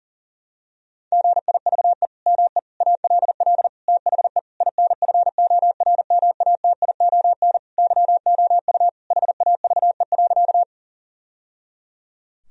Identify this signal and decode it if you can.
Morse code, heard as a keyed tone, about 40 wpm, 700 Hz. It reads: GIVE ME ALL THE INFORMATION YOU HAVE.
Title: Sentences from Top 1000 Words - Review Part 3 - 40wpm